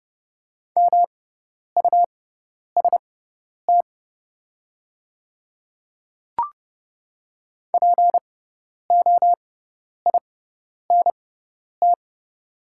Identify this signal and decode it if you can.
Morse code, heard as a keyed tone, about 30 wpm, spaced as for 10 wpm, 700 Hz. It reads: MUST POINT
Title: 6x Character Spacing - Top 200 Words - 30wpm